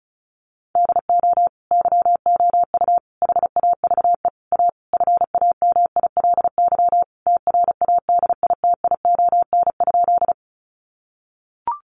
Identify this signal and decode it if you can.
Morse code, heard as a keyed tone, about 34 wpm, 700 Hz. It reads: DO YOU HAVE A FAMILY TRADITION?
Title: Sentences from Top 1500 Words 35wpm